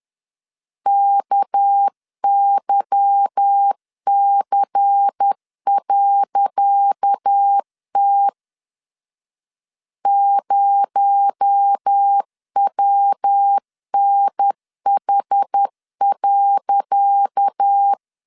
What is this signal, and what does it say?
Morse code, heard as a keyed tone, about 10 wpm, 800 Hz. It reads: KYC.T 0WNH.
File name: LEZIONI / 10 SIGNS / 10C DOT 10wpm 2940